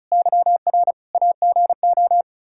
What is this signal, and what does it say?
Morse code, heard as a keyed tone, about 36 wpm, 700 Hz. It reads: YR AGO
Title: Practice QSO - Exchange 3 - More Info - 35wpm